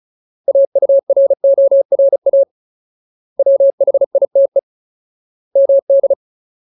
Morse code, heard as a keyed tone, about 34 wpm, 550 Hz. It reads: AURORA WHITE MD
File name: Ultimate Word List / 35WPM Ultimate